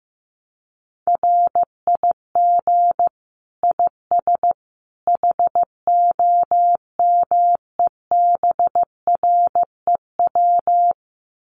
Morse code, heard as a keyed tone, about 15 wpm, 700 Hz. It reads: RIG IS HOMEBREW
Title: Common QSO Phrases 15wpm